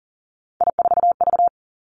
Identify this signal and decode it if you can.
Morse code, heard as a keyed tone, about 38 wpm, 690 Hz. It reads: I4V
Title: Three Challenging Letter-Number Combinations 40wpm